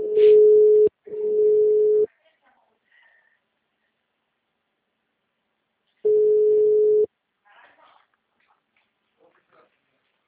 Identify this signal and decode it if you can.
Ringback tone (Europe, 425 Hz), with speech.
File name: out- -2004- - - .121